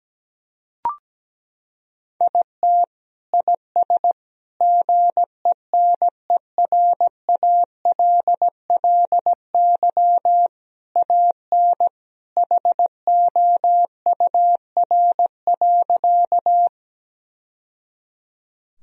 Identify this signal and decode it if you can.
Morse code, heard as a keyed tone, about 17 wpm, 700 Hz. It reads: IT IS GENERALLY AN HOUR.
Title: Sentences from Top 1000 Words - Review Part 4 - 17wpm